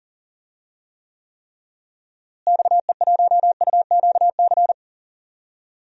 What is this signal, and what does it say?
Morse code, heard as a keyed tone, about 40 wpm, 700 Hz. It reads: XE1UQC
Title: DXCC-100 Call Signs - 6 Characters - Repeated 3x - 40wpm